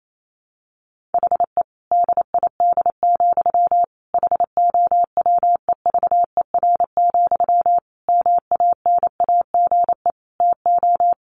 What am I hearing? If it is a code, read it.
Morse code, heard as a keyed tone, about 28 wpm, 700 Hz. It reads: HE DID, HOWEVER, MANAGE TO
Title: Sentences from Top 1400 Words 28wpm